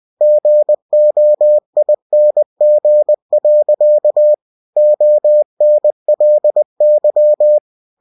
Morse code, heard as a keyed tone, about 20 wpm, 600 Hz. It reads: GOING. ONLY
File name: winnie-20wpm